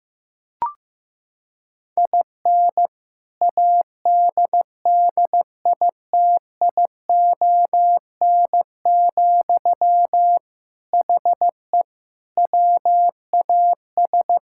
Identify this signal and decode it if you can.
Morse code, heard as a keyed tone, about 15 wpm, 700 Hz. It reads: IN ADDITION, HE WAS
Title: Sentences from Top 1000 Words - Review Part 6 - 15wpm